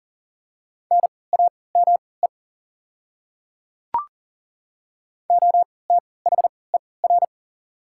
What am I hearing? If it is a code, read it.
Morse code, heard as a keyed tone, about 40 wpm, spaced as for 23 wpm, 690 Hz. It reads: NAME OTHER
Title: 3x Character Spacing - Top 200 Words - 40wpm